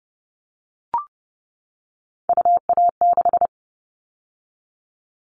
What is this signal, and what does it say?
Morse code, heard as a keyed tone, about 30 wpm, 700 Hz. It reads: UA6
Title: All Cut Numbers - 3-Digit 30wpm